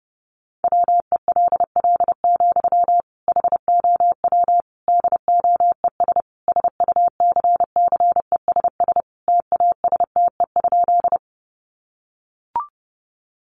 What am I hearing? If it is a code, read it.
Morse code, heard as a keyed tone, about 30 wpm, 700 Hz. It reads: WELL, HOW DOES SUCCESS TASTE?
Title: Sentences from Top 1100 Words 30wpm